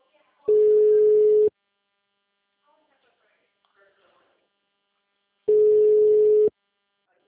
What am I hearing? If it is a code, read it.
Ringback tone (Europe, 425 Hz), with speech.